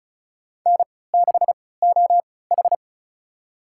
Morse code, heard as a keyed tone, about 36 wpm, spaced as for 20 wpm, 700 Hz. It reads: N6OH
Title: USA Call Signs - 4 Characters - 3x Character Spacing - 35wpm